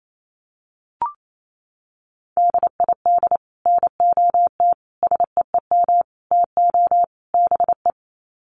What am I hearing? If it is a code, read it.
Morse code, heard as a keyed tone, about 28 wpm, 700 Hz. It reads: DID NOT SEEM TO BE
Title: Sets of 5 Words Encore 28wpm